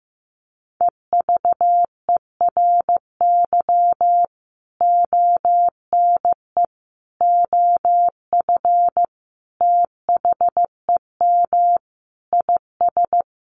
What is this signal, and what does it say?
Morse code, heard as a keyed tone, about 15 wpm, 700 Hz. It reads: EVERY ONE OF THEM IS
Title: Sentences from Top 1100 Words 15wpm